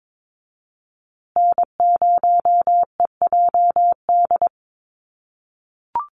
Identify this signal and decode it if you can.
Morse code, heard as a keyed tone, about 22 wpm, 700 Hz. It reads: N0EJD
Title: USA Call Signs Repeated 2x 22wpm